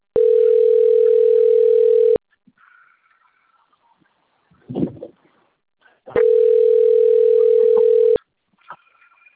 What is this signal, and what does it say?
Ringback tone (North America), 440 + 480 Hz.
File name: rg-1600-unknown- - - .23991